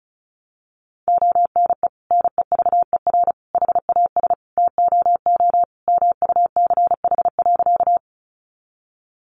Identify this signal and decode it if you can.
Morse code, heard as a keyed tone, about 34 wpm, 700 Hz. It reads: ONE NEVER HAS TOO MUCH.
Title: Sentences from Top 300 Words 35wpm